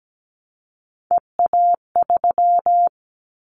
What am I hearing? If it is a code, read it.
Morse code, heard as a keyed tone, about 17 wpm, 700 Hz. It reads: EA3
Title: Callsign Prefixes for Top 25 DXCC - 17wpm